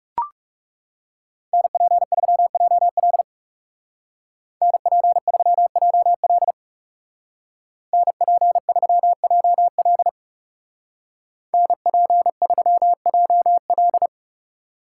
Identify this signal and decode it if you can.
Morse code, heard as a keyed tone, about 36 wpm, 700 Hz. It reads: NP3JL NP3JL NP3JL NP3JL
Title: Speed Racing - USA 2x2 Call Signs 30wpm